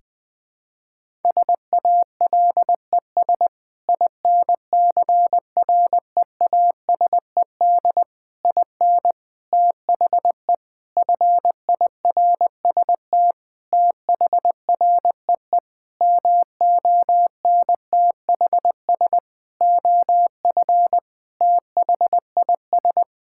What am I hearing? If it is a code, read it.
Morse code, heard as a keyed tone, about 21 wpm, 700 Hz. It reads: SALES INCREASED IN THE FIRST THREE MONTHS OF THIS
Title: Sentences with 8-Word Phrases - 20wpm